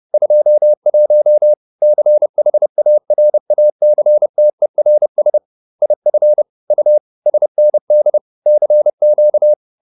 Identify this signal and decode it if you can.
Morse code, heard as a keyed tone, about 30 wpm, 600 Hz. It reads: 21 CHARACTERS IF U SND CQ